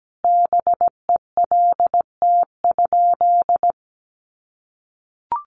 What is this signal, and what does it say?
Morse code, heard as a keyed tone, about 17 wpm, 700 Hz. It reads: BELT?
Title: Sentences from Top 1400 Words 17wpm